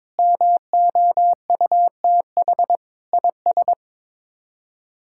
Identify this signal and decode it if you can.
Morse code, heard as a keyed tone, about 22 wpm, 700 Hz. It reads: MOUTH IS
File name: practice / 22wpm / Sets of 7 Words - 22wpm